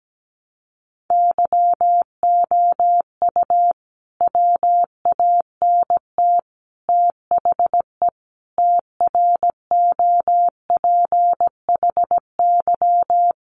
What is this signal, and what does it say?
Morse code, heard as a keyed tone, about 17 wpm, 700 Hz. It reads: YOU WANT THE TROPHY
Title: A Sound Of Thunder 17wpm